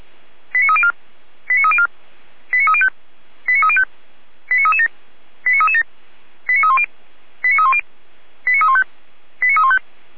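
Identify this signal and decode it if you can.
Five-tone selective call (ZVEI), heard as quick runs of five tones, tones of about 70 ms. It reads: 89304 89304 89305 89305 89337 89337 89310 89310 89315 89315
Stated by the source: ZVEI five-tone selective call received on SDR by fire brigade Feuerwehr Orlamünde.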